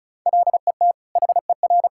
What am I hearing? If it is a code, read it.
Morse code, heard as a keyed tone, about 36 wpm, 700 Hz. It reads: LET HER
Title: Sentences from Top 1000 Words - Review Part 6 - 35wpm